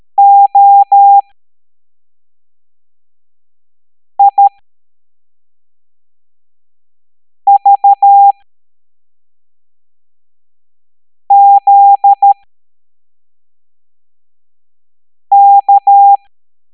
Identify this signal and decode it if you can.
Morse code, heard as a keyed tone, about 13 wpm, spaced as for 3 wpm, 800 Hz. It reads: OIVZK
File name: hamdata / L09-EX04